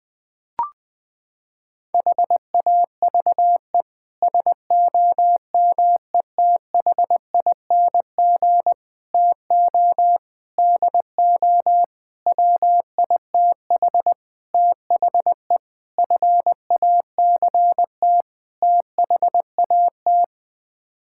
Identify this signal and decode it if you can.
Morse code, heard as a keyed tone, about 21 wpm, 700 Hz. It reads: HAVE SOMETHING TO DO WITH THE FACT THAT
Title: Sets of 8 Words - 20wpm